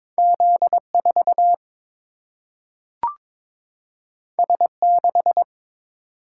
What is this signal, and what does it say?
Morse code, heard as a keyed tone, about 22 wpm, 700 Hz. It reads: Z 4 S 6
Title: Single-Letter Single-Number 22wpm